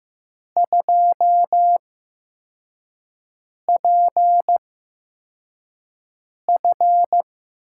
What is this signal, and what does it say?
Morse code, heard as a keyed tone, about 15 wpm, spaced as for 4 wpm, 700 Hz. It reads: 2PF
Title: USA Call Signs - 6 Characters - 8x Character Spacing - 15wpm